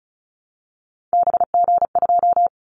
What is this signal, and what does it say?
Morse code, heard as a keyed tone, about 34 wpm, 700 Hz. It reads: BG2
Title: Callsign Prefixes for Top 100 DXCC - 35wpm